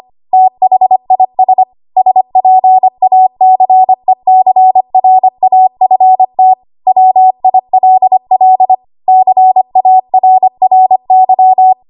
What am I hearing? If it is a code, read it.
Morse code, heard as a keyed tone, about 25 wpm, 750 Hz. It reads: THIS SPACECRAFT WILL CARRY
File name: Archive / 25 WPM / 25WPM